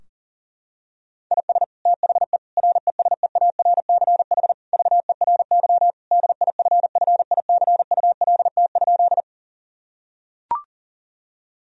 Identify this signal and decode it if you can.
Morse code, heard as a keyed tone, about 40 wpm, 700 Hz. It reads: IS THE RESEARCH VERY DIFFICULT?
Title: Sentences from Top 1000 Words - Review Part 4 - 40wpm